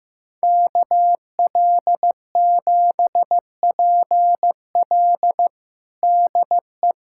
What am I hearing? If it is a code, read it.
Morse code, heard as a keyed tone, about 15 wpm, 700 Hz. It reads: KL7PL DE
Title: USA Callsign DE Callsign - 5 Characters - 15wpm